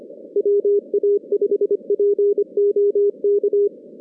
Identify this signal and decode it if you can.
Morse code, heard as a keyed tone, about 25 wpm, 410 Hz. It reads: WA5POK
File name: 2016 JAN CW NAQP / 2016 Jan naqp k5ot - 0-10